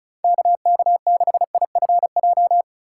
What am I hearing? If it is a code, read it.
Morse code, heard as a keyed tone, about 34 wpm, 700 Hz. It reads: KK6IFJ
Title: USA Call Signs - 6 Characters - 3x Character Spacing - 35wpm